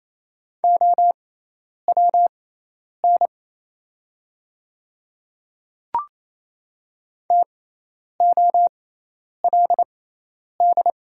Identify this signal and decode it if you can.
Morse code, heard as a keyed tone, about 28 wpm, spaced as for 10 wpm, 700 Hz. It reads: OWN TOLD
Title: 6x Character Spacing - Top 500 Words - 28wpm